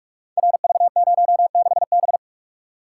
Morse code, heard as a keyed tone, about 44 wpm, 700 Hz. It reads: RV06B
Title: Code Groups - 5 Letter:Numbers - 45wpm